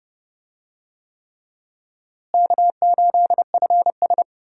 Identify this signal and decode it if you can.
Morse code, heard as a keyed tone, about 32 wpm, 700 Hz. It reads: K8FS
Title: USA Call Signs - 4 Characters - Repeated 3x - 30wpm